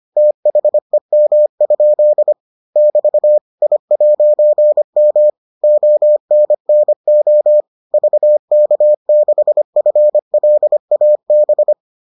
Morse code, heard as a keyed tone, about 25 wpm, 600 Hz. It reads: THEM? <BT> I'M ONNO VK6FLAB